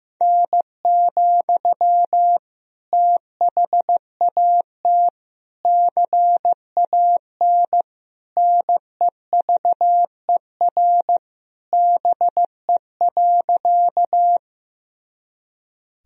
Morse code, heard as a keyed tone, about 16 wpm, 700 Hz. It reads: N, THAT CAN NEVER BE.
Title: Sentences from Top 300 Words and Abbreviations 15wpm